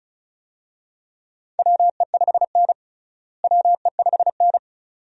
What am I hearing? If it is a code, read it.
Morse code, heard as a keyed tone, about 36 wpm, 700 Hz. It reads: WE5N WE5N
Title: USA Call Signs - 4 Characters - Repeated 2x - 35wpm